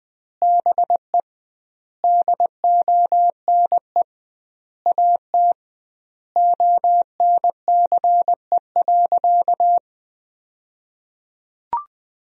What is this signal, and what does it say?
Morse code, heard as a keyed tone, about 20 wpm, 700 Hz. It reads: BE DONE AT ONCE.
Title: 2x Word-Spacing - Sentences from Top 500 Words - 20wpm